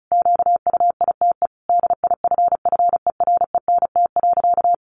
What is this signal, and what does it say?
Morse code, heard as a keyed tone, about 34 wpm, 700 Hz. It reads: QUITE DIFFERENT.
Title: Sentences from Top 700 Words 35wpm